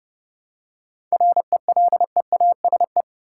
Morse code, heard as a keyed tone, about 30 wpm, 700 Hz. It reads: RELEASE
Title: Seven Letter Words - 30wpm